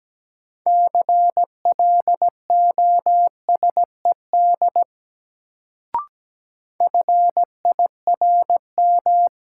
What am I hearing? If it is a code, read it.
Morse code, heard as a keyed tone, about 17 wpm, 700 Hz. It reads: CLOSED FIRM
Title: Sentences from Top 1400 Words - New Word Drill Alt - 17wpm